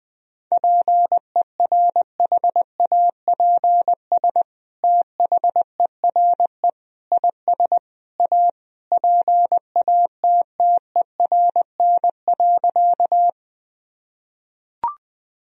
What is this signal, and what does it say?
Morse code, heard as a keyed tone, about 20 wpm, 700 Hz. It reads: PERHAPS THERE IS A PATTERN.
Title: Sentences from Top 1000 Words - Review Part 2 - 20wpm